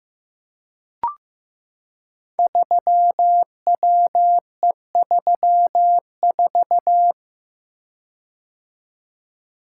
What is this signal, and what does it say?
Morse code, heard as a keyed tone, about 15 wpm, 700 Hz. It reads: 3WE34
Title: Code Groups - 5 Letter:Numbers - 15wpm